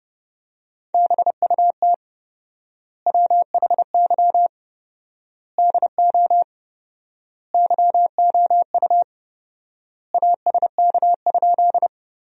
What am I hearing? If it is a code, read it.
Morse code, heard as a keyed tone, about 30 wpm, 700 Hz. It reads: BUT WHY DO YOU ASK?
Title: 4x Word-Spacing - Sentences from Top 200 Words - 30wpm